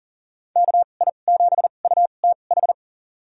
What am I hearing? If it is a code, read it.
Morse code, heard as a keyed tone, about 40 wpm, spaced as for 29 wpm, 700 Hz. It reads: KI7UTH
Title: USA Call Signs - 2x Character Spacing - 40wpm